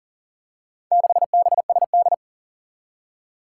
Morse code, heard as a keyed tone, about 39 wpm, 700 Hz. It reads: 6BSD
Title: Four Challenging Letter-Number Combinations 40wpm